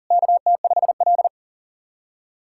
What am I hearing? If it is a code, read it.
Morse code, heard as a keyed tone, about 44 wpm, 700 Hz. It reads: KT5L
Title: SOTA QSO Practice 40wpm